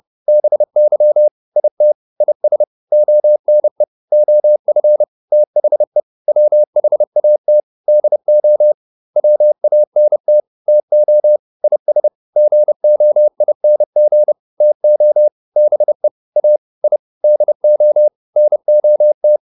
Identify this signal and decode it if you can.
Morse code, heard as a keyed tone, about 30 wpm, 600 Hz. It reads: BY IT IS ONE OF THE WHAT DO WANT TO IS GOING TO BE A I DO NOT